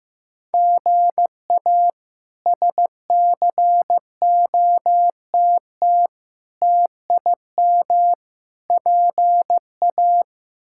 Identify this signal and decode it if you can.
Morse code, heard as a keyed tone, about 15 wpm, 700 Hz. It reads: GA SCOTT TIM PA
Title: K1USN SST QSO Practice - 15wpm